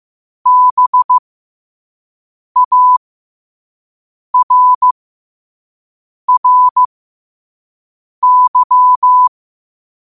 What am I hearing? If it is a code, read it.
Morse code, heard as a keyed tone, about 15 wpm, spaced as for 5 wpm, 1000 Hz. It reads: BARRY
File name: qotdfiles / 5wpm